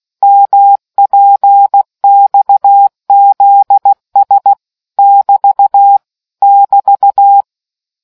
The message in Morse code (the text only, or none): MPXZS <BT> <BT>